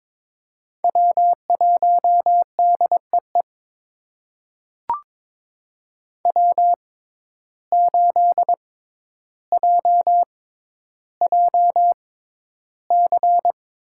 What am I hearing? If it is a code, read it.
Morse code, heard as a keyed tone, about 21 wpm, spaced as for 11 wpm, 700 Hz. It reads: W1DEE W8JJC
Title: USA Call Signs - 5 Characters - 6x Character Spacing - 22wpm